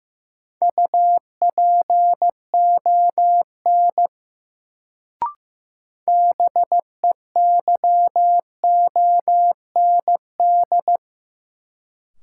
Morse code, heard as a keyed tone, about 16 wpm, 700 Hz. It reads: UPON BEYOND
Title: Sentences from Top 1000 Words - Review Part 6 - New Word Drill - 15wpm